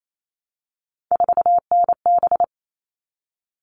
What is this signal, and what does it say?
Morse code, heard as a keyed tone, about 28 wpm, 700 Hz. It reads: 4NB
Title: All Cut Numbers - 3-Digit 28wpm